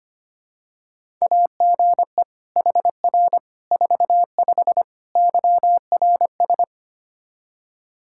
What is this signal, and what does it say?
Morse code, heard as a keyed tone, about 25 wpm, 700 Hz. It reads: AGE HR 45 YRS
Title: Practice QSO - Exchange 3 - More Info - 25wpm